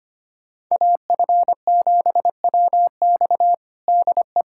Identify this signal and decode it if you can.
Morse code, heard as a keyed tone, about 25 wpm, 700 Hz. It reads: AF7WX DE
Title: USA Callsign DE Callsign - 5 Characters - 25wpm